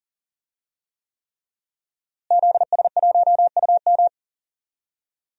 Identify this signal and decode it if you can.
Morse code, heard as a keyed tone, about 40 wpm, 700 Hz. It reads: ZS1UM
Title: DXCC-100 Call Signs - 5 Characters - Repeated 3x - 40wpm